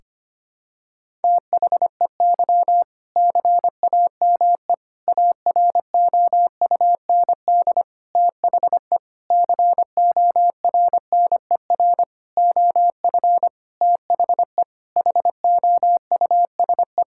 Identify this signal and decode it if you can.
Morse code, heard as a keyed tone, about 25 wpm, 700 Hz. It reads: THEY CAME AROUND THE CORNER OF THE HOUSE
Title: Sentences from Top 1600 Words 25wpm